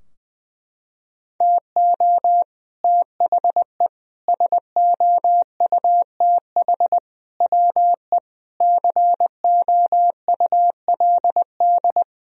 Morse code, heard as a keyed tone, about 20 wpm, 700 Hz. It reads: TO THE SOUTH WE COULD
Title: Sentences from Top 1000 Words - Review Part 2 - 20wpm